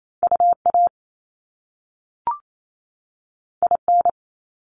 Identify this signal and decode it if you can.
Morse code, heard as a keyed tone, about 28 wpm, 700 Hz. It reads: UA IN
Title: Top Two Letter Combinations in Words 28wpm